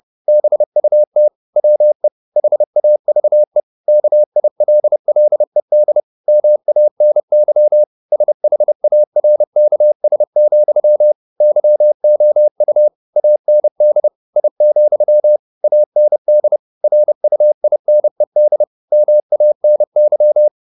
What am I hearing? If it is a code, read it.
Morse code, heard as a keyed tone, about 30 wpm, 600 Hz. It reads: BUT WE HAVE KILLED MANY SHARKS, YOU AND I, AND RUINED MANY